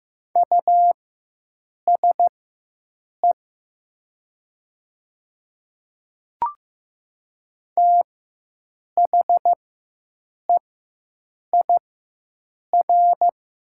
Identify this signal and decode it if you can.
Morse code, heard as a keyed tone, about 15 wpm, spaced as for 7 wpm, 700 Hz. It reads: USE THEIR
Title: 4x Character Spacing - Top 100 Words - 15wpm